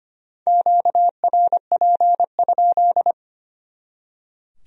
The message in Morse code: QRP?

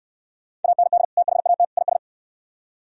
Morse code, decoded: AND THAT IS

